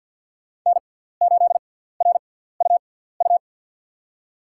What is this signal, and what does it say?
Morse code, heard as a keyed tone, about 51 wpm, spaced as for 17 wpm, 700 Hz. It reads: N8RUU
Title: USA Call Signs - 5 Characters - 6x Character Spacing - 50wpm